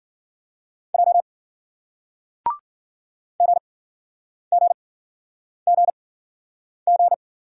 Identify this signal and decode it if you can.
Morse code, heard as a keyed tone, about 54 wpm, spaced as for 9 wpm, 700 Hz. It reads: J GGGG